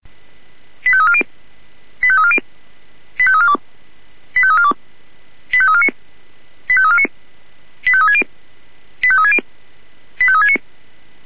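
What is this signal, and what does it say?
Five-tone selective call (ZVEI), heard as quick runs of five tones, tones of about 70 ms. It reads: 85340 85340 85342 85342 85349 85349 85360 85360 85369
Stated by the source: ZVEI five-tone selective call received on an SDR.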